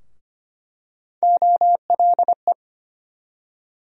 Morse code, heard as a keyed tone, about 25 wpm, 700 Hz. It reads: OLE
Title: USA Call Sign Postfixes - 3 Letters 25wpm